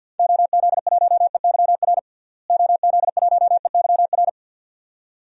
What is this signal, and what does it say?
Morse code, heard as a keyed tone, about 51 wpm, 700 Hz. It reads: OZ1EYR OZ1EYR
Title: DXCC-100 Call Signs - 6 Characters - Repeated 2x - 50wpm